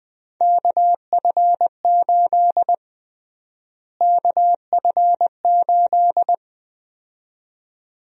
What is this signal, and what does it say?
Morse code, heard as a keyed tone, about 21 wpm, 700 Hz. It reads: KF8 KF8
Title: USA Call Sign Prefixes - Repeated 2x - 20wpm